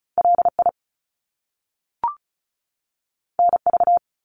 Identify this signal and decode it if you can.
Morse code, heard as a keyed tone, about 36 wpm, 700 Hz. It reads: LI NV
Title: Top Two Letter Combinations in Words 35wpm